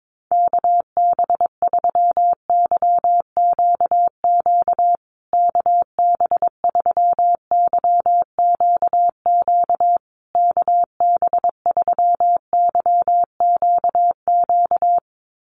Morse code, heard as a keyed tone, about 22 wpm, 700 Hz. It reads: KB3YQQ KB3YQQ KB3YQQ
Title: USA Call Signs Repeated 3x 22wpm